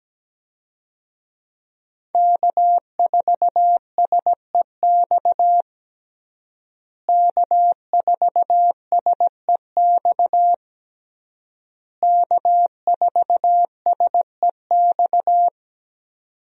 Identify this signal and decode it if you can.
Morse code, heard as a keyed tone, about 17 wpm, 700 Hz. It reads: K4SEX K4SEX K4SEX
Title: USA Call Signs - 5 Characters - Repeated 3x - 17wpm